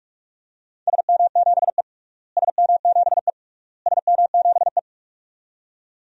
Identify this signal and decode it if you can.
Morse code, heard as a keyed tone, about 41 wpm, 700 Hz. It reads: SM7E SM7E SM7E
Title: DXCC-100 Call Signs - 4 Characters - Repeated 3x - 45wpm